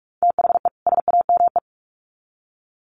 Morse code, heard as a keyed tone, about 44 wpm, 700 Hz. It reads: THE SAME
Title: Sets of 2 Words 45wpm